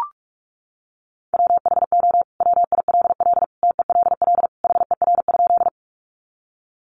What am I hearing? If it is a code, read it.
Morse code, heard as a keyed tone, about 44 wpm, 700 Hz. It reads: WHO WILL TELL HER?
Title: Sentences from Top 200 Words 45wpm